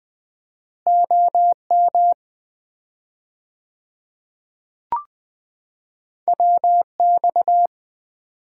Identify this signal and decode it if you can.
Morse code, heard as a keyed tone, about 20 wpm, 700 Hz. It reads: OM WX
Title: Common QSO Phrases - 4x Word-Spacing - 20wpm